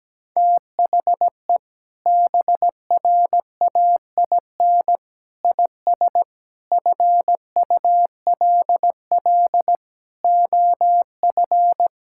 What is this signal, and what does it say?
Morse code, heard as a keyed tone, about 17 wpm, 700 Hz. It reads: THE BRAIN IS FULL OF
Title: Sentences from Top 1000 Words - Review Part 6 - 17wpm